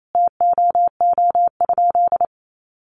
Morse code, heard as a keyed tone, about 28 wpm, 700 Hz. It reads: TOO?
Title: Sentences from Top 1500 Words 28wpm